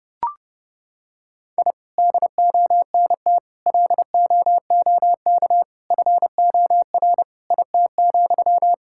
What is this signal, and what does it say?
Morse code, heard as a keyed tone, about 30 wpm, 700 Hz. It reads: I DONT LOOK FOR IT,
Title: Sentences from Top 1000 Words - Review Part 5 - 30wpm